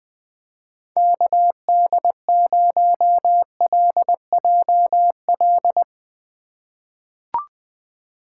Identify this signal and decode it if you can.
Morse code, heard as a keyed tone, about 20 wpm, 700 Hz. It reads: KD0LJL
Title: USA Call Signs - 6 Characters - 3x Character Spacing - 20wpm